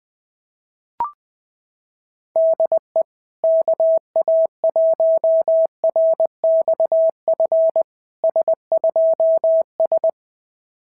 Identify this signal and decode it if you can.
Morse code, heard as a keyed tone, about 20 wpm, 650 Hz. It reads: DE KA1RXF S2S
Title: SOTA QSO Practice 20wpm